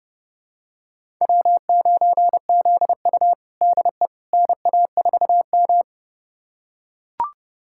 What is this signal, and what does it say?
Morse code, heard as a keyed tone, about 30 wpm, 700 Hz. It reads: W9ZU DE NA4M
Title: USA Callsign DE Callsign - 4 Characters - 30wpm